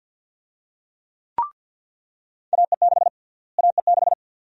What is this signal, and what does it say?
Morse code, heard as a keyed tone, about 45 wpm, 700 Hz. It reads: AE6 AE6
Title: USA Call Sign Prefixes - Repeated 2x - 50wpm